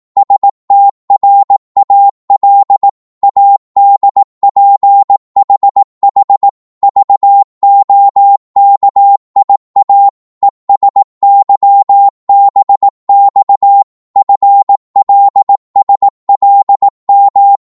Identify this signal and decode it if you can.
Morse code, heard as a keyed tone, about 18 wpm, 800 Hz. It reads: STRAL ADPHH VOKIA ESYBX FLSLM
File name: Morse / ltr-18wpm